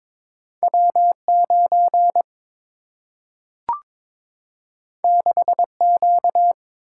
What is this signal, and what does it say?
Morse code, heard as a keyed tone, about 22 wpm, 700 Hz. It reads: W 9 6 Q
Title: Single-Letter Single-Number 22wpm